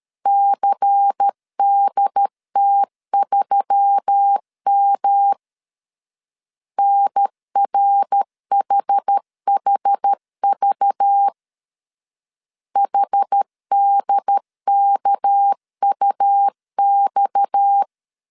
Morse code, heard as a keyed tone, about 13 wpm, 800 Hz. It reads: CDT3M NRHHV HDKUX
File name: CW / DET / 1 6 1500C S A 12wpm 3440